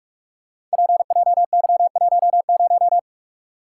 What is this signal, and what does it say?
Morse code, heard as a keyed tone, about 44 wpm, 700 Hz. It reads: PJY10
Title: Code Groups - 5 Letter:Numbers - 45wpm